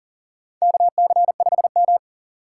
Morse code, heard as a keyed tone, about 40 wpm, 700 Hz. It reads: KC5M
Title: USA Call Signs - 4 Characters - 4x Character Spacing - 40wpm